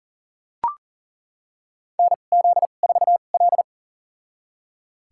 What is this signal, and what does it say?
Morse code, heard as a keyed tone, about 37 wpm, spaced as for 28 wpm, 700 Hz. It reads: NZ4L
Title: USA Call Signs - 2x Character Spacing - 40wpm